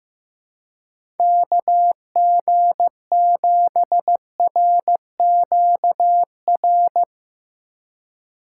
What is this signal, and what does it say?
Morse code, heard as a keyed tone, about 15 wpm, 700 Hz. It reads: KG7RQR
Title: USA Call Signs - 6 Characters - 2x Character Spacing - 15wpm